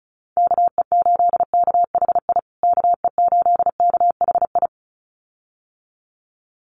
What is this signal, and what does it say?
Morse code, heard as a keyed tone, about 34 wpm, 700 Hz. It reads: KE8KHI KE8KHI
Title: USA Call Signs Repeated 2x 35wpm